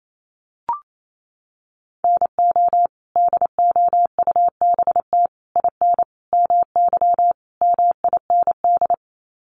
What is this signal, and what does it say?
Morse code, heard as a keyed tone, about 28 wpm, 700 Hz. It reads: NO DOUBT IN MY MIND
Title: Sets of 5 Words Encore2 28wpm